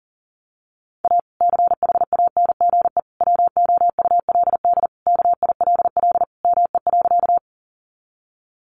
Morse code, heard as a keyed tone, about 40 wpm, 700 Hz. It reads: A CHANGE WOULD KILL ME.
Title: Sentences from Top 1400 Words 40wpm